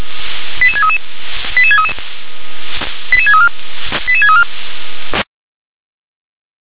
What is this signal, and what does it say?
Five-tone selective call (ZVEI), heard as quick runs of five tones, tones of about 70 ms. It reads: 88633 88633 88634 88634